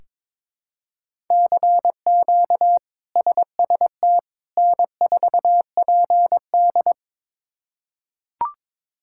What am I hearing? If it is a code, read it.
Morse code, heard as a keyed tone, about 22 wpm, 700 Hz. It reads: CQ SST N4PD
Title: K1USN SST QSO Practice - 22wpm